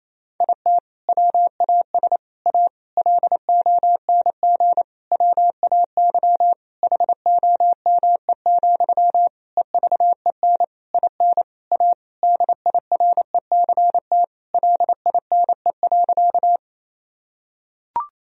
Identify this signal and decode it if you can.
Morse code, heard as a keyed tone, about 27 wpm, 700 Hz. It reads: IT WAS A LONG WAY HOME, EVEN IN A DIRECT LINE.
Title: Sentences from Top 1000 Words - Review Part 2 - 28wpm